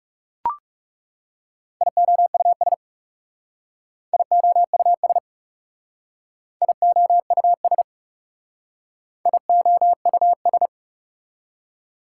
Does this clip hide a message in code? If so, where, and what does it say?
Yes: Morse code from 1.8 to 10.7 s, heard as a keyed tone, about 38 wpm, 700 Hz. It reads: IOUS IOUS IOUS IOUS